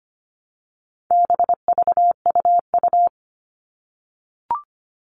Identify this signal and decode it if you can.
Morse code, heard as a keyed tone, about 25 wpm, 700 Hz. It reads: BVUU
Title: All Cut Numbers - 4-Digit 25wpm